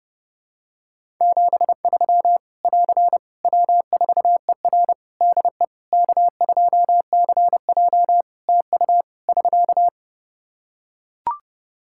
Morse code, heard as a keyed tone, about 30 wpm, 700 Hz. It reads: 73 <AR> W4ER DE K2CJ TU <SK>
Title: Practice QSO - Exchange 4 - Ending - 30wpm